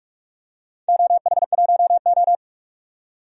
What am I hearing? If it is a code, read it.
Morse code, heard as a keyed tone, about 44 wpm, 690 Hz. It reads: OH1O
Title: DXCC-100 Call Signs - 4 Characters - 45wpm